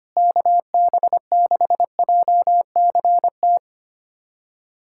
Morse code, heard as a keyed tone, about 25 wpm, 700 Hz. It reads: KB6JCT
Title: USA Callsign DE Callsign - 6 Characters - 25wpm